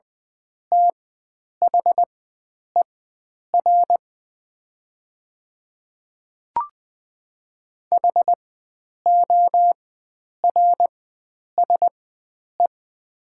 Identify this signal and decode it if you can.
Morse code, heard as a keyed tone, about 20 wpm, spaced as for 9 wpm, 700 Hz. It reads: THER HORSE
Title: 4x Character Spacing - Top 500 Words - 20wpm